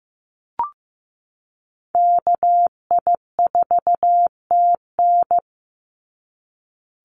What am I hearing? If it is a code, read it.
Morse code, heard as a keyed tone, about 15 wpm, 700 Hz. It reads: KI4TN
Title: NA SOTA Activator Callsigns 15wpm